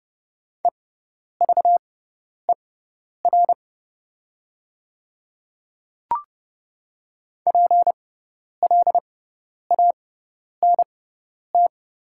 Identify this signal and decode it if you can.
Morse code, heard as a keyed tone, about 30 wpm, spaced as for 10 wpm, 690 Hz. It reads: EVER PLANT
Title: 6x Character Spacing - Top 500 Words - 30wpm